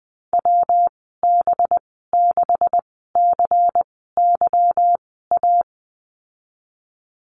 Morse code, heard as a keyed tone, about 19 wpm, spaced as for 14 wpm, 700 Hz. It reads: WB6CYA
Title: 186 Lesson 30g - Letter b - Call Signs 20wpm